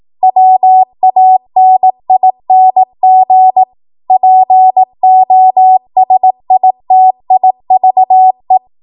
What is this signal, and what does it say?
Morse code, heard as a keyed tone, about 18 wpm, 750 Hz. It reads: WANING POSITIVE
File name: Morse / 18WPM